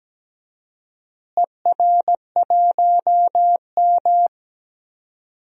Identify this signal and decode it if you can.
Morse code, heard as a keyed tone, about 17 wpm, 700 Hz. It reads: ER1M
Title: DXCC-100 Call Signs - 4 Characters - Repeated 3x - 17wpm